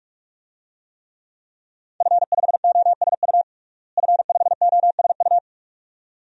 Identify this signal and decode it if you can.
Morse code, heard as a keyed tone, about 46 wpm, 700 Hz. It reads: F5OSU F5OSU
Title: DXCC-100 Call Signs - 5 Characters - Repeated 2x - 45wpm